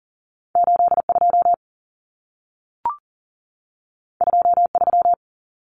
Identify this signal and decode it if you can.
Morse code, heard as a keyed tone, about 40 wpm, 700 Hz. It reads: 82 23
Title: Numbers - 2-Digit 40wpm